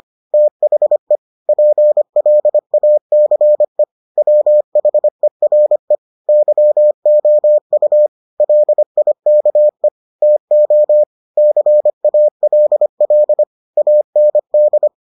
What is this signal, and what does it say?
Morse code, heard as a keyed tone, about 25 wpm, 600 Hz. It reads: THE PLACE WHERE YOU LIKE TO CALL AND